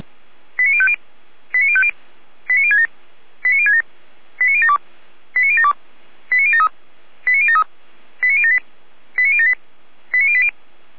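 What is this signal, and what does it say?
Five-tone selective call (ZVEI), heard as quick runs of five tones, tones of about 70 ms. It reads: 89050 89050 89067 89067 89072 89072 89073 89073 89078 89078 89080
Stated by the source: ZVEI five-tone call received on SDR.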